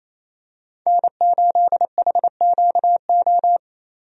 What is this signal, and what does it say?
Morse code, heard as a keyed tone, about 27 wpm, 700 Hz. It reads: N8HQO
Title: USA Call Signs - 5 Characters - Repeated 3x - 28wpm